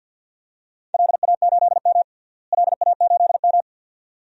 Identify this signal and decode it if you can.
Morse code, heard as a keyed tone, about 47 wpm, 700 Hz. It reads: LA8M LA8M
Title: DXCC-100 Call Signs - 4 Characters - Repeated 2x - 50wpm